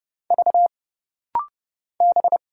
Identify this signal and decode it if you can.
Morse code, heard as a keyed tone, about 29 wpm, 700 Hz. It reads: V B
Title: Single Letters - Rapid-Fire 30wpm